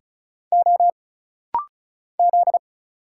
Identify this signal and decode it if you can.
Morse code, heard as a keyed tone, about 34 wpm, 700 Hz. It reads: O Z
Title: Single Letters - Rapid-Fire 35wpm